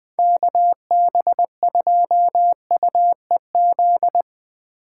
Morse code, heard as a keyed tone, about 20 wpm, 700 Hz. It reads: KB2UEZ
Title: USA Call Signs - 6 Characters - 8x Character Spacing - 20wpm